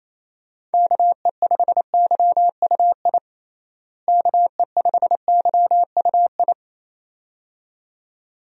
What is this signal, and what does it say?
Morse code, heard as a keyed tone, about 27 wpm, 700 Hz. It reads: KE5YUI KE5YUI
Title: USA Call Signs - 6 Characters - Repeated 2x - 28wpm